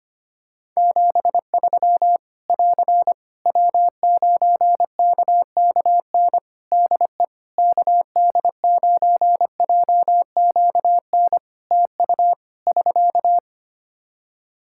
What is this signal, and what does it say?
Morse code, heard as a keyed tone, about 25 wpm, 700 Hz. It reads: 73 <AR> W9KKN DE KD9JQN TU <SK>
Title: Practice QSO - Exchange 4 - Ending - 25wpm